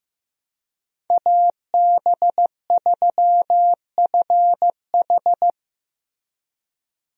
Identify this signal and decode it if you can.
Morse code, heard as a keyed tone, about 15 wpm, 700 Hz. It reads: AB3FH
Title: USA Call Signs - 5 Characters - Repeated 2x - 15wpm